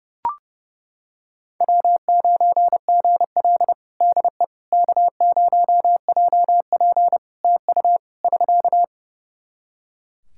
Morse code, heard as a keyed tone, about 32 wpm, 700 Hz. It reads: W9GL DE K0JP TU <SK>
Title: Practice QSO - Exchange 4 - Ending - 30wpm